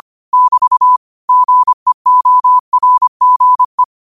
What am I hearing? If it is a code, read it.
Morse code, heard as a keyed tone, about 25 wpm, 1000 Hz. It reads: <BT> GEORGE
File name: qotdfiles / 25wpm